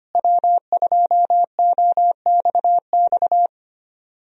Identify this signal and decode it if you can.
Morse code, heard as a keyed tone, about 25 wpm, 700 Hz. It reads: W2OXX
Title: USA Call Signs - 5 Characters - Repeated 2x - 25wpm